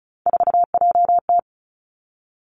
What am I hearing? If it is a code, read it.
Morse code, heard as a keyed tone, about 36 wpm, 700 Hz. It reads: 4JT
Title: Three Challenging Letter-Number Combinations 35wpm